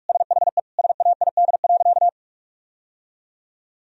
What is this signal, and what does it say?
Morse code, heard as a keyed tone, about 41 wpm, 700 Hz. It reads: SHE SAID.
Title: Sentences from Top 1000 Words - Review Part 4 - 45wpm